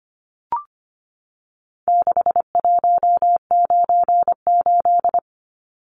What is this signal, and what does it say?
Morse code, heard as a keyed tone, about 25 wpm, 700 Hz. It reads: 6198
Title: Numbers - 4-Digit 25wpm